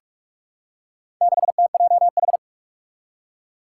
Morse code, heard as a keyed tone, about 43 wpm, 700 Hz. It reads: 6TJH
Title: Four Challenging Letter-Number Combinations 45wpm